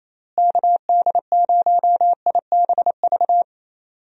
Morse code, heard as a keyed tone, about 28 wpm, 700 Hz. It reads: KD0IBV
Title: USA Call Signs - 6 Characters - Repeated 2x - 28wpm